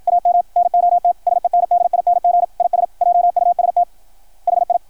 Morse code, heard as a keyed tone, about 53 wpm, 700 Hz, with radio noise. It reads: AM NOT SENDING IS JUST HI
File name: archived / CRSnet